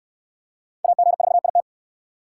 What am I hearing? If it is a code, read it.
Morse code, heard as a keyed tone, about 66 wpm, 700 Hz. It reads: AG4IT